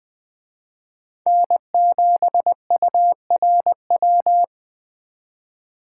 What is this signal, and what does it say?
Morse code, heard as a keyed tone, about 24 wpm, 700 Hz. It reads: N7URW
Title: Speed Racing - USA 1x3 Call Signs 20wpm